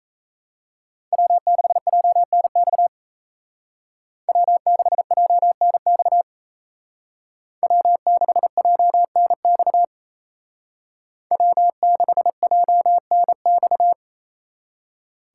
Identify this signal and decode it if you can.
Morse code, heard as a keyed tone, about 36 wpm, 700 Hz. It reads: W6JNX W6JNX W6JNX W6JNX